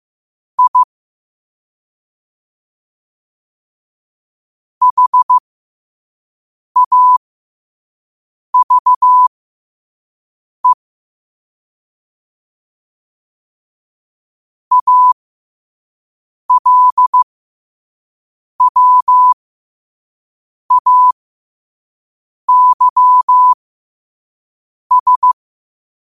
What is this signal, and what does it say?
Morse code, heard as a keyed tone, about 15 wpm, spaced as for 5 wpm, 1000 Hz. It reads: I HAVE ALWAYS
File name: qotdfiles / 5wpm